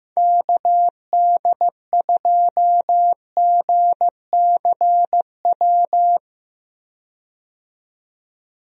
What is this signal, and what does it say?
Morse code, heard as a keyed tone, about 15 wpm, 700 Hz. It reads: KD2GCW
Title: USA Call Signs - 6 Characters - 15wpm